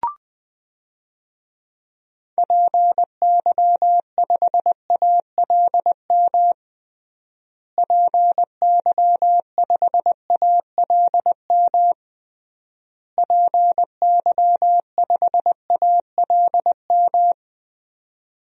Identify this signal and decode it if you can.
Morse code, heard as a keyed tone, about 21 wpm, 700 Hz. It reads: PY5ALM PY5ALM PY5ALM
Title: DXCC-100 Call Signs - 6 Characters - Repeated 3x - 20wpm